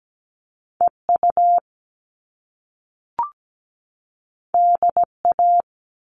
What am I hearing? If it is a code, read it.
Morse code, heard as a keyed tone, about 17 wpm, 700 Hz. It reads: EU DA
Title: All Cut Numbers - 2-Digit 17wpm